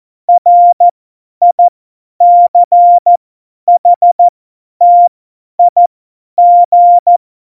Morse code, heard as a keyed tone, about 14 wpm, spaced as for 10 wpm, 700 Hz. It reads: RICHTIG
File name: Morsekurs / Witze / Witz 29. Januar 2019, 50 F70